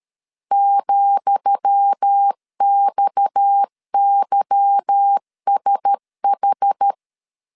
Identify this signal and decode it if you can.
Morse code, heard as a keyed tone, about 13 wpm, 800 Hz. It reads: ,XYSH
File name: LEZIONI / 10 SIGNS / 10D COMA 12wpm 2544